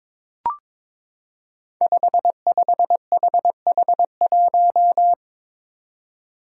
Morse code, heard as a keyed tone, about 22 wpm, 700 Hz. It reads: 55HH1
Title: Five Challenging Letter-Number Combinations 22wpm